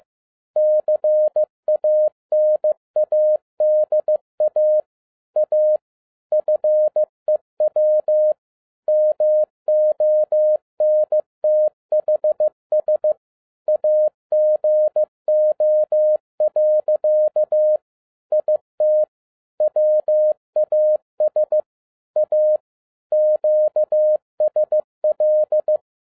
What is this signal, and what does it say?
Morse code, heard as a keyed tone, about 15 wpm, 610 Hz. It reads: CANADA A FEW MONTHS AGO. IT WAS A QSL
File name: 2011-10-mp3 / n07-A DOT 15wpm.mp3